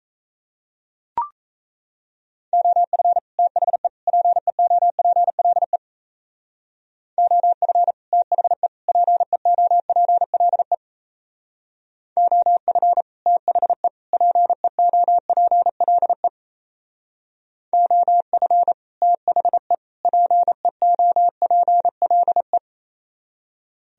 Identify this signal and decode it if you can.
Morse code, heard as a keyed tone, about 34 wpm, 700 Hz. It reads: OF THE PEOPLE OF THE PEOPLE OF THE PEOPLE OF THE PEOPLE
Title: Speed-Racing - Sets of 3 Words 28wpm